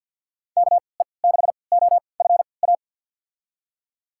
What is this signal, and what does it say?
Morse code, heard as a keyed tone, about 52 wpm, spaced as for 29 wpm, 700 Hz. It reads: KE6OFA
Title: USA Call Signs - 6 Characters - 3x Character Spacing - 50wpm